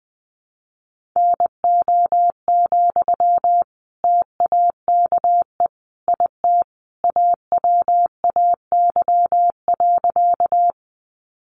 Morse code, heard as a keyed tone, about 20 wpm, 700 Hz. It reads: NO, TAKE IT AWAY.
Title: Sentences from Top 600 Words 20wpm